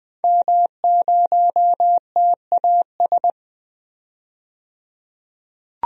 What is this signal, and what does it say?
Morse code, heard as a keyed tone, about 20 wpm, 700 Hz. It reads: M0TAS
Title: DXCC-100 Call Signs - 5 Characters - 20wpm